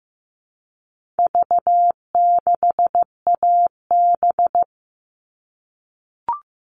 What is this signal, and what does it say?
Morse code, heard as a keyed tone, about 15 wpm, 700 Hz. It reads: V6AB
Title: All Cut Numbers - 4-Digit 15wpm